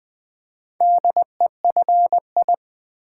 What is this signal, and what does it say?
Morse code, heard as a keyed tone, about 20 wpm, 690 Hz. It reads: DEFI
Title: Build A Word - 2 Letters At A Time - 6-Letter Words - 20wpm